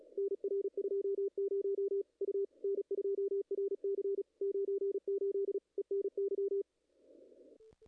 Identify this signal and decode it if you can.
Morse code, heard as a keyed tone, about 35 wpm, 410 Hz, with radio noise. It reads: NR 20 U N2RC 98 ENY
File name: contests / ss-cw-k5tr-op-n5rz -0000